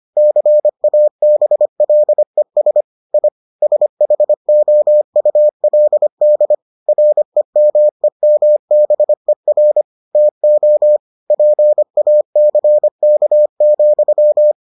Morse code, heard as a keyed tone, about 25 wpm, 600 Hz. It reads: CABLES I SHOULD REMEMBER TO PACK,